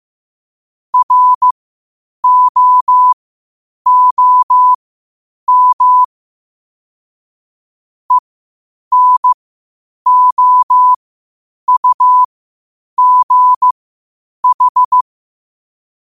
Morse code, heard as a keyed tone, about 15 wpm, spaced as for 8 wpm, 1000 Hz. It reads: ROOM ENOUGH
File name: qotdfiles / 8wpm